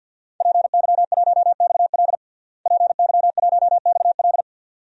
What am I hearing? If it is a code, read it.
Morse code, heard as a keyed tone, about 47 wpm, 700 Hz. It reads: PY1XL PY1XL
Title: DXCC-100 Call Signs - 5 Characters - Repeated 2x - 50wpm